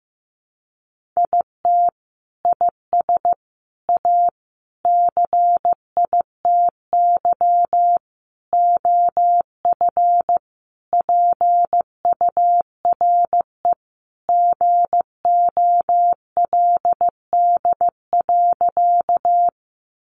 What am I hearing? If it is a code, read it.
Morse code, heard as a keyed tone, about 15 wpm, 700 Hz. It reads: IT IS A CITY OF PURE GOLD.
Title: Sentences from Top 1500 Words 15wpm